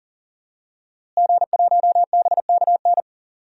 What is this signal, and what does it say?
Morse code, heard as a keyed tone, about 40 wpm, 700 Hz. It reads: G1BKN
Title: DXCC-100 Call Signs - 5 Characters - Repeated 3x - 40wpm